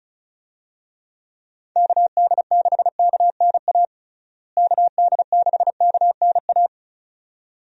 Morse code, heard as a keyed tone, about 36 wpm, 700 Hz. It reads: KD6KNA KD6KNA
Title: USA Call Signs - 6 Characters - Repeated 2x - 35wpm